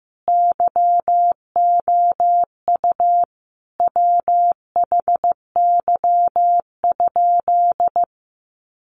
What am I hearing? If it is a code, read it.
Morse code, heard as a keyed tone, about 15 wpm, 700 Hz. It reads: YOU WHY?
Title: Sentences from Top 200 Words 15wpm